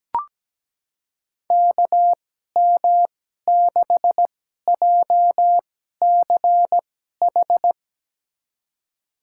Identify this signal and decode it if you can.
Morse code, heard as a keyed tone, about 17 wpm, spaced as for 12 wpm, 700 Hz. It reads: KM6JCH
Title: USA Call Signs - 6 Characters - 2x Character Spacing - 17wpm